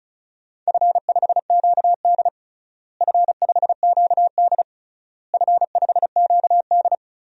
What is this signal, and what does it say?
Morse code, heard as a keyed tone, about 36 wpm, 700 Hz. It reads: F5QD F5QD F5QD
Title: DXCC-100 Call Signs - 4 Characters - Repeated 3x - 35wpm